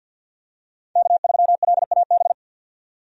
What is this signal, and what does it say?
Morse code, heard as a keyed tone, about 52 wpm, 700 Hz. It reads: K3LAB
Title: USA Call Signs - 5 Characters - 50wpm